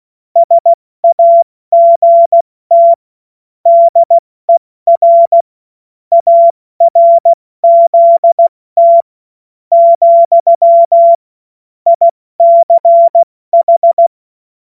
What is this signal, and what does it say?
Morse code, heard as a keyed tone, about 16 wpm, spaced as for 13 wpm, 680 Hz. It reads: SAGT DER ARZT, ICH
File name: Morsekurs / Witze / Witz 06. September 2022 70 F80